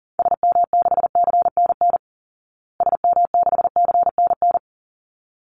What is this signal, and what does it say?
Morse code, heard as a keyed tone, about 37 wpm, 700 Hz. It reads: SM6CNN SM6CNN
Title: Call Signs Repeated 3x - 40wpm